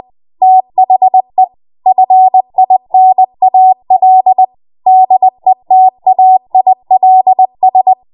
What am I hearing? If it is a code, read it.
Morse code, heard as a keyed tone, about 20 wpm, 750 Hz. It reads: THE FINAL DETAILS
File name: Morse / 20WPM